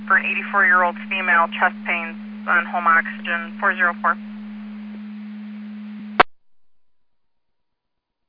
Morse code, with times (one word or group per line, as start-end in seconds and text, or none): none